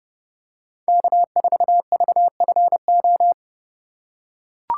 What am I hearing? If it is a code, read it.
Morse code, heard as a keyed tone, about 30 wpm, 700 Hz. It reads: K4VFO
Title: USA Call Signs - 3x Character Spacing - 30wpm